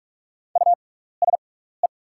Morse code, heard as a keyed tone, about 44 wpm, spaced as for 15 wpm, 700 Hz. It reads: USE
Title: 6x Character Spacing - Top 200 Words - 45wpm